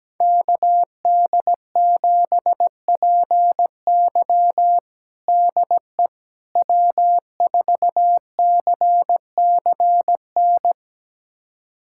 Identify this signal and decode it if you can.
Morse code, heard as a keyed tone, about 17 wpm, 700 Hz. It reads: KD7PY DE W4CCN
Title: USA Callsign DE Callsign - 5 Characters - 17wpm